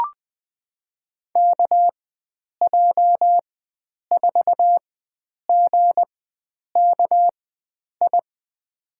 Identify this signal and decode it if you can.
Morse code, heard as a keyed tone, about 20 wpm, spaced as for 9 wpm, 700 Hz. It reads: KJ4GKI
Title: USA Call Signs - 6 Characters - 4x Character Spacing - 20wpm